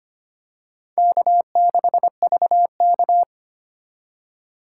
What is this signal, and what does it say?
Morse code, heard as a keyed tone, about 25 wpm, 700 Hz. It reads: K6VK
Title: USA Call Signs - 4 Characters - 3x Character Spacing - 25wpm